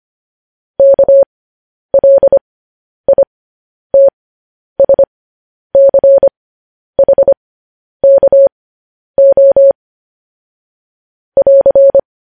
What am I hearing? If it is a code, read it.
Morse code, heard as a keyed tone, about 25 wpm, spaced as for 10 wpm, 550 Hz. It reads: KLITSCHKO <AR>
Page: news headlines in morse code at 10 wpm